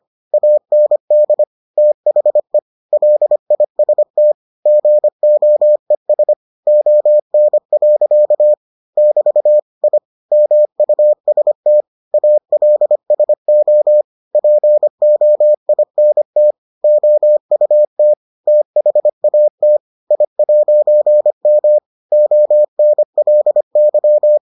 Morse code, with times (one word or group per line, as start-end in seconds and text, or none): 0.3-1.4 AND
1.8-2.6 THE
2.9-4.3 LIST
4.7-6.3 GOES
6.7-8.5 ON.
9.0-9.6 <BT>
9.8-10.0 I
10.3-11.8 MUST
12.1-14.0 ALSO
14.4-16.5 POINT
16.8-18.1 OUT
18.5-19.8 THAT
20.1-21.8 I'M
22.1-24.4 ONLY